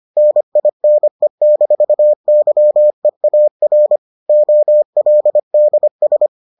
Morse code, heard as a keyed tone, about 25 wpm, 600 Hz. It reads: NINE-YEAR OLDS